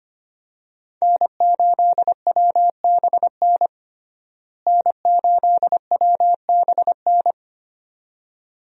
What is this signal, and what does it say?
Morse code, heard as a keyed tone, about 25 wpm, 700 Hz. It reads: N8WBN N8WBN
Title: USA Call Signs - 5 Characters - Repeated 2x - 25wpm